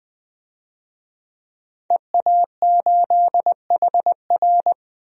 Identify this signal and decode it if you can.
Morse code, heard as a keyed tone, about 20 wpm, 700 Hz. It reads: EA8HR
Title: DXCC-100 Call Signs - 5 Characters - 20wpm